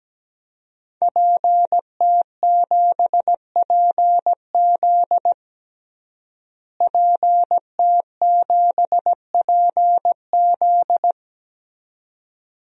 Morse code, heard as a keyed tone, about 17 wpm, 700 Hz. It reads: PT7PZ PT7PZ
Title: DXCC-100 Call Signs - 5 Characters - Repeated 2x - 17wpm